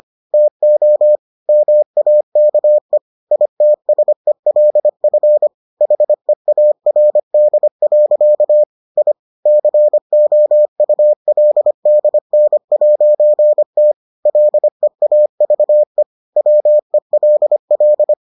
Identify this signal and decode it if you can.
Morse code, heard as a keyed tone, about 25 wpm, 600 Hz. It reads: TO MAKE ITSELF HEARD. I COULDN'T LEAVE WELL